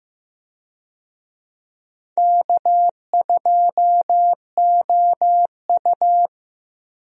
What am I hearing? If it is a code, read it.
Morse code, heard as a keyed tone, about 16 wpm, 700 Hz. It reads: K2OU
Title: USA Call Signs - 4 Characters - Repeated 2x - 15wpm